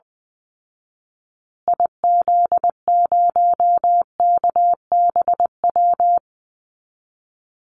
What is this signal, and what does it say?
Morse code, heard as a keyed tone, about 20 wpm, 700 Hz. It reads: IZ0KBW
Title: Call Signs Repeated 2x - 20wpm